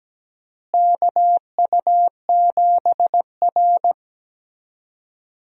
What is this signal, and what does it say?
Morse code, heard as a keyed tone, about 17 wpm, 700 Hz. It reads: KU7R
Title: USA Call Signs - 4 Characters - Repeated 2x - 17wpm